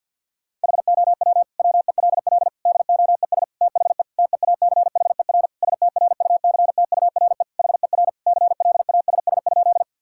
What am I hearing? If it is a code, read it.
Morse code, heard as a keyed tone, about 50 wpm, 700 Hz. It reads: HOW WELL DOES THE TEACHER STRUCTURE HER CLASS?
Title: Sentences from Top 1000 Words - Review Part 6 - 50wpm